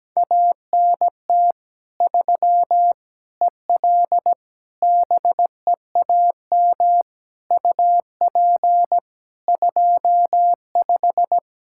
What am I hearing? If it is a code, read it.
Morse code, heard as a keyed tone, about 17 wpm, 700 Hz. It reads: ANT 3 EL BEAM UP 25
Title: Practice QSO - Exchange 2 - Info - 17wpm